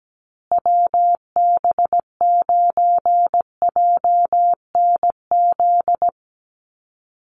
Morse code, heard as a keyed tone, about 17 wpm, 700 Hz. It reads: WB9JNZ
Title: USA Call Signs Repeated 2x 17wpm